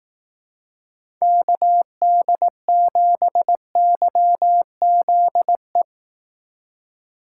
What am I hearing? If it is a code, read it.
Morse code, heard as a keyed tone, about 18 wpm, 700 Hz. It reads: KD7YZE
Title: Speed Racing - USA Call Signs 15wpm